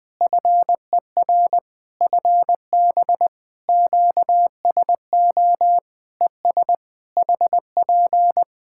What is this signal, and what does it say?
Morse code, heard as a keyed tone, about 20 wpm, 700 Hz. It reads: FER FB QSO ES HP
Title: Practice QSO - Exchange 4 - Ending - 20wpm